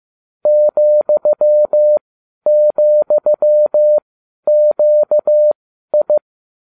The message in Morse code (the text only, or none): ,,QI